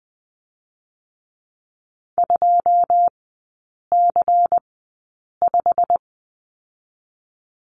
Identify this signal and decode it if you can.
Morse code, heard as a keyed tone, about 20 wpm, spaced as for 8 wpm, 690 Hz. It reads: 2C5
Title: 093 Lesson 16e - Letter c - Three Characters 20wpm